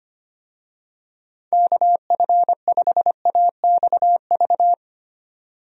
Morse code, heard as a keyed tone, about 25 wpm, 700 Hz. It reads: KF5AXV
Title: USA Call Signs - 6 Characters - 25wpm